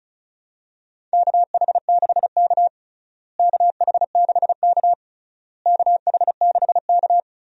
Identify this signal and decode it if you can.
Morse code, heard as a keyed tone, about 36 wpm, 700 Hz. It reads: KH6K KH6K KH6K
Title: DXCC-100 Call Signs - 4 Characters - Repeated 3x - 35wpm